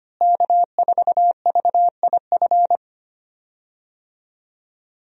Morse code, heard as a keyed tone, about 25 wpm, 700 Hz. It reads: K4VIF
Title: USA Call Signs - 5 Characters - Repeated 2x - 25wpm